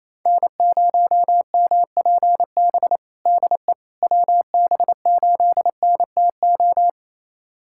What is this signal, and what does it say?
Morse code, heard as a keyed tone, about 28 wpm, 700 Hz. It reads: N0MPB DE WB8NTO
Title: USA Callsign DE Callsign 28wpm